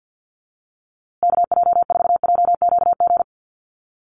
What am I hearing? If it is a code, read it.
Morse code, heard as a keyed tone, about 48 wpm, 700 Hz. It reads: KJ4JQG
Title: USA Call Signs Repeated 3x 50wpm